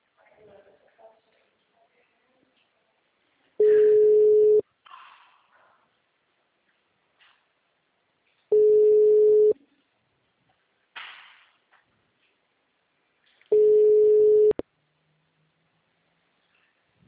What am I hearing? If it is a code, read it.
Ringback tone (Europe, 425 Hz), with speech.